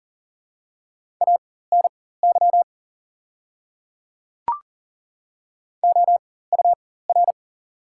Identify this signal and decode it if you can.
Morse code, heard as a keyed tone, about 40 wpm, spaced as for 19 wpm, 700 Hz. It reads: ANY OUR
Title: 4x Character Spacing - Top 200 Words - 40wpm